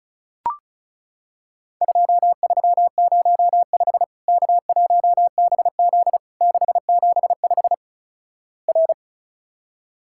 Morse code, heard as a keyed tone, about 36 wpm, 700 Hz. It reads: 2305 K1BZ 675 R
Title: Worked All Europe DX Contest - QTC Exchange - 35wpm